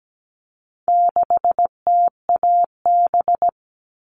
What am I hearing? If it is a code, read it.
Morse code, heard as a keyed tone, about 17 wpm, 700 Hz. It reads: 6TAB
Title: All Cut Numbers - 4-Digit 17wpm